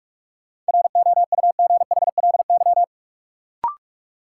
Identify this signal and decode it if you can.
Morse code, heard as a keyed tone, about 45 wpm, 700 Hz. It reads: ROUGHLY
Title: Sentences from Top 1500 Words - New Word Drill - 45wpm